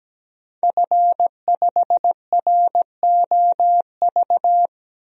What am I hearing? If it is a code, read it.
Morse code, heard as a keyed tone, about 17 wpm, 700 Hz. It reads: F5ROV
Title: DXCC-100 Call Signs - 5 Characters - 17wpm